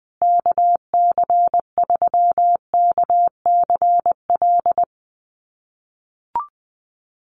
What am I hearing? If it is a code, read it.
Morse code, heard as a keyed tone, about 20 wpm, 700 Hz. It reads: KC3KCL
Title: USA Call Signs Repeated 3x 20wpm